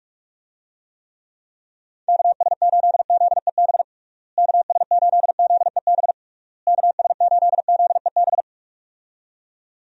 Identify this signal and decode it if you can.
Morse code, heard as a keyed tone, about 46 wpm, 700 Hz. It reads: KS8ZEB KS8ZEB KS8ZEB
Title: USA Call Signs - 6 Characters - Repeated 3x - 45wpm